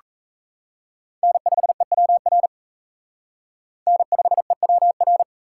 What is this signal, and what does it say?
Morse code, heard as a keyed tone, about 36 wpm, 700 Hz. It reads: N5EWR N5EWR